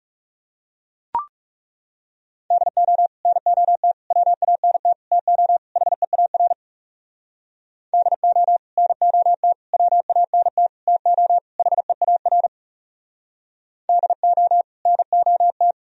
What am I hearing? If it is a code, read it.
Morse code, heard as a keyed tone, about 37 wpm, 700 Hz. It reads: DO NOT WANT TO HEAR DO NOT WANT TO HEAR DO NOT
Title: Speed-Racing - Sets of 5 Words 30wpm